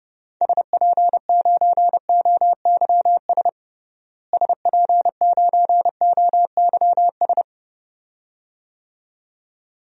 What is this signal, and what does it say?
Morse code, heard as a keyed tone, about 30 wpm, 700 Hz. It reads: SP9OYS SP9OYS
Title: DXCC-100 Call Signs - 6 Characters - Repeated 2x - 30wpm